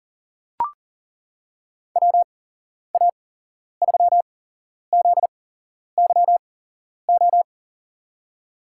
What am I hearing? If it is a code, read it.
Morse code, heard as a keyed tone, about 38 wpm, spaced as for 11 wpm, 700 Hz. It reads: WA3ZYO
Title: USA Call Signs - 6 Characters - 8x Character Spacing - 40wpm